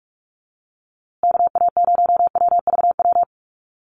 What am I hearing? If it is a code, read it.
Morse code, heard as a keyed tone, about 44 wpm, 700 Hz. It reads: KA0WVW
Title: USA Call Signs Repeated 3x 45wpm